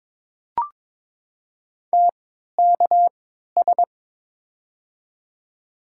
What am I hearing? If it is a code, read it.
Morse code, heard as a keyed tone, about 21 wpm, spaced as for 12 wpm, 700 Hz. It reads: TKS
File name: Top 90 QSO Elements - 3x Character Spacing - 22wpm